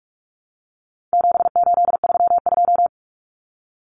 Morse code, heard as a keyed tone, about 44 wpm, 700 Hz. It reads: 7832
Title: Common Cut Numbers - 4-Digit 45wpm